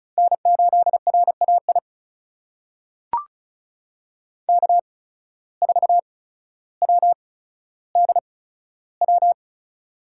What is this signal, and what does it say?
Morse code, heard as a keyed tone, about 36 wpm, spaced as for 15 wpm, 700 Hz. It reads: N8RAI K4WDW
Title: USA Call Signs - 5 Characters - 8x Character Spacing - 35wpm